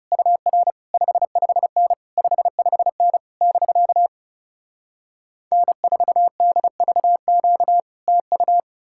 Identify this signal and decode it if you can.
Morse code, heard as a keyed tone, about 40 wpm, 700 Hz. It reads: UR 55N 55N <BK> N4DVQ TU